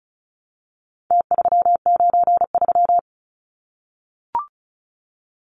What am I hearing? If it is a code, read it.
Morse code, heard as a keyed tone, about 34 wpm, 700 Hz. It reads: T393
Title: Numbers - Contest Style - 4-Digit 35wpm